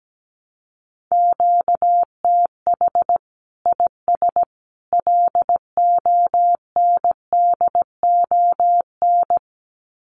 Morse code, heard as a keyed tone, about 17 wpm, 700 Hz. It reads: QTH IS LONDON
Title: Common QSO Phrases 17wpm